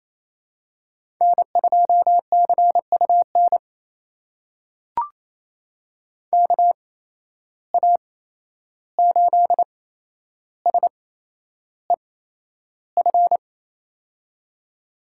Morse code, heard as a keyed tone, about 28 wpm, spaced as for 11 wpm, 700 Hz. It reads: N2CUN KA8SEF
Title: USA Call Signs - 8x Character Spacing - 28wpm